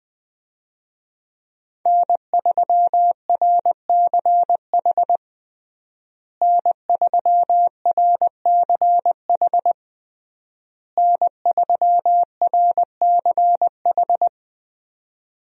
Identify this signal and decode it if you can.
Morse code, heard as a keyed tone, about 20 wpm, 700 Hz. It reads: N3RCH N3RCH N3RCH
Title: USA Call Signs - 5 Characters - Repeated 3x - 20wpm